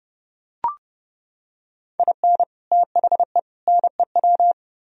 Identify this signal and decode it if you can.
Morse code, heard as a keyed tone, about 29 wpm, 700 Hz. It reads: IN THE NEW